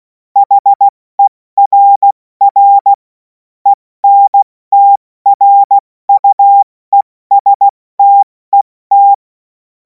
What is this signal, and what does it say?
Morse code, heard as a keyed tone, about 16 wpm, 800 Hz. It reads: HERR ENTRUESTET
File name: Morsekurs / Witze / Witz 16. Juni 2026 70 F80